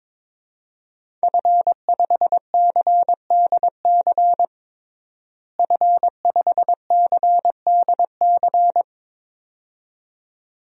Morse code, heard as a keyed tone, about 22 wpm, 700 Hz. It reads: F5CDC F5CDC
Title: DXCC-100 Call Signs - 5 Characters - Repeated 2x - 22wpm